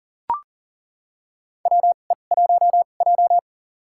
Morse code, heard as a keyed tone, about 40 wpm, spaced as for 29 wpm, 700 Hz. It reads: WE1J
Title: USA Call Signs - 2x Character Spacing - 40wpm